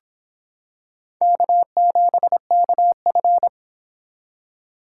Morse code, heard as a keyed tone, about 26 wpm, 700 Hz. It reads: K7KF